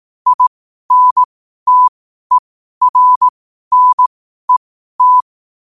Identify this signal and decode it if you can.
Morse code, heard as a keyed tone, about 18 wpm, spaced as for 12 wpm, 1000 Hz. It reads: INTERNET
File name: qotdfiles / 12wpm